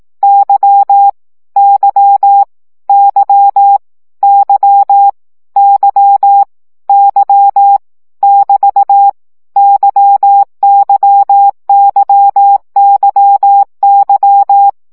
Morse code, heard as a keyed tone, about 18 wpm, 800 Hz. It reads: Y Y Y Y Y Y <BT> YYYYY
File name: CW gyakorlas y 15 baud